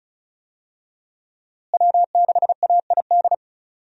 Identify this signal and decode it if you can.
Morse code, heard as a keyed tone, about 36 wpm, 700 Hz. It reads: W6AID
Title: USA Call Signs - 5 Characters - Repeated 3x - 35wpm